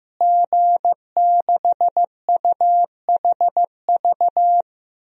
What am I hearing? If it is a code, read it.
Morse code, heard as a keyed tone, about 15 wpm, 700 Hz. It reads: G6UHV
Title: DXCC-100 Call Signs - 5 Characters - Repeated 3x - 15wpm